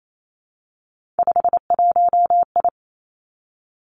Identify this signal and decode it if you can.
Morse code, heard as a keyed tone, about 28 wpm, 700 Hz. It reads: 51I
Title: Three Challenging Letter-Number Combinations 28wpm